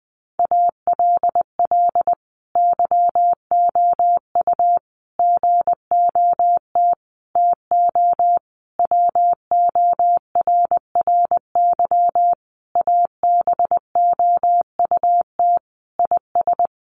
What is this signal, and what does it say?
Morse code, heard as a keyed tone, about 20 wpm, 700 Hz. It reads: ALL YOU GOT TO WORRY ABOUT IS
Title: A Sound Of Thunder 20wpm